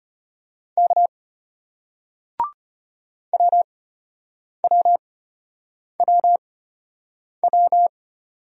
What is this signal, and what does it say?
Morse code, heard as a keyed tone, about 34 wpm, spaced as for 8 wpm, 700 Hz. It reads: K WWWW